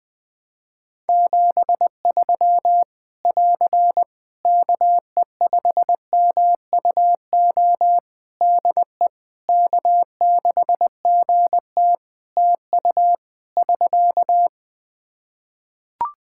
Morse code, heard as a keyed tone, about 20 wpm, 700 Hz. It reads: 73 <AR> KE5MUO DE K6GT TU <SK>
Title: Practice QSO - Exchange 4 - Ending - 20wpm